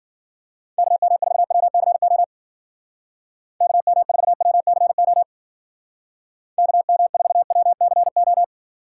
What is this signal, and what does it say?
Morse code, heard as a keyed tone, about 50 wpm, 700 Hz. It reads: KM4WCO KM4WCO KM4WCO